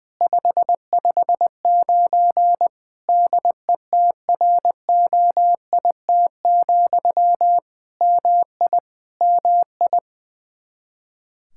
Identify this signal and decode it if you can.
Morse code, heard as a keyed tone, about 20 wpm, 700 Hz. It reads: 559 DETROIT, MI MI
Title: Practice QSO - Exchange 1 - Beginning - 20wpm